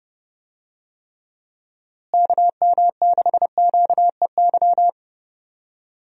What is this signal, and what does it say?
Morse code, heard as a keyed tone, about 30 wpm, 700 Hz. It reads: KM6QEY
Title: USA Call Signs - 6 Characters - Repeated 2x - 30wpm